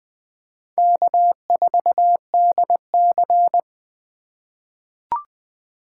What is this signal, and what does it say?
Morse code, heard as a keyed tone, about 21 wpm, 700 Hz. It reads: K4DC
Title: USA Call Signs - 2x Character Spacing - 20wpm